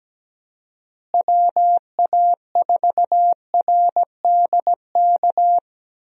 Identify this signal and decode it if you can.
Morse code, heard as a keyed tone, about 17 wpm, 700 Hz. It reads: WA4RDK
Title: USA Call Signs - 6 Characters - Repeated 3x - 17wpm